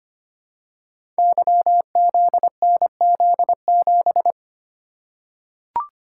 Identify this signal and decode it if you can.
Morse code, heard as a keyed tone, about 25 wpm, 700 Hz. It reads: YZNZ7
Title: Code Groups - 5 Letter:Numbers - 25wpm